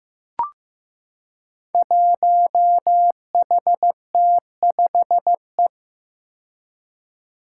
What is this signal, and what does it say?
Morse code, heard as a keyed tone, about 15 wpm, 700 Hz. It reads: 1HT5E
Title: Five Challenging Letter-Number Combinations 15wpm